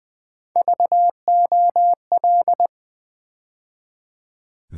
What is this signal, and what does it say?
Morse code, heard as a keyed tone, about 20 wpm, 700 Hz. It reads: VOL
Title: Build A Word - 3 Letters At A Time - 9-Letter Words - 20wpm